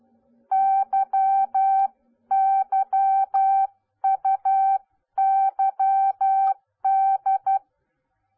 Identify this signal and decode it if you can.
Morse code, heard as a keyed tone, about 12 wpm, 780 Hz. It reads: YYUYD